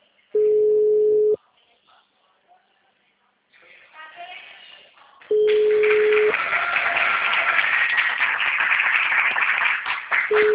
Ringback tone (Europe, 425 Hz).